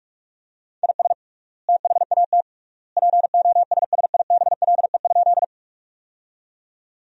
Morse code, heard as a keyed tone, about 46 wpm, 700 Hz. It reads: IS THAT POSSIBLE?
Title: 3x Word-Spacing - Sentences from Top 500 Words - 45wpm